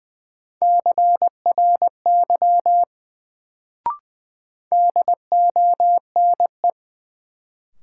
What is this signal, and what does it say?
Morse code, heard as a keyed tone, about 20 wpm, 700 Hz. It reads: CRY DONE
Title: Sentences from Top 500 Words - New Word Drill - 20wpm